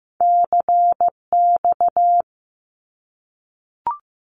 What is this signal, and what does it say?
Morse code, heard as a keyed tone, about 15 wpm, 700 Hz. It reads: CX
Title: Abbreviations 15wpm